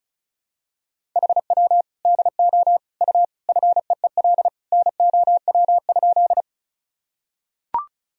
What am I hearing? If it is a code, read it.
Morse code, heard as a keyed tone, about 36 wpm, 700 Hz. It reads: HW DO U FEEL NOW?
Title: Sentences from Top 400 Words and Abbreviations 35wpm